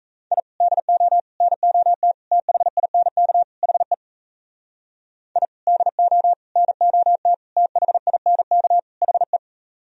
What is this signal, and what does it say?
Morse code, heard as a keyed tone, about 38 wpm, 700 Hz. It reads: I DO NOT THINK HE I DO NOT THINK HE